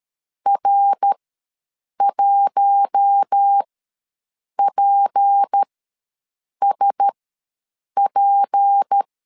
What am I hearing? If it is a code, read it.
Morse code, heard as a keyed tone, about 13 wpm, spaced as for 7 wpm, 800 Hz. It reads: R1PSP
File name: LEZIONI / 6 PKX0 / 6 P ws 12wpm 3117